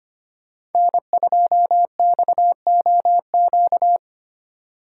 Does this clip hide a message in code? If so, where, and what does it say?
Yes: Morse code from 0.8 to 4.0 s, heard as a keyed tone, about 33 wpm, 700 Hz. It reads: N2XOQ